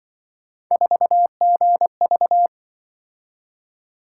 Morse code, heard as a keyed tone, about 23 wpm, 700 Hz. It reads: 4GV
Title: Speed-Racing - Three Letter-Number Combinations - 20wpm